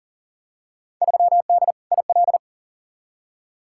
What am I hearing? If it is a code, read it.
Morse code, heard as a keyed tone, about 40 wpm, 700 Hz. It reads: 3D IL
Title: Field Day Exchanges Encore - 40wpm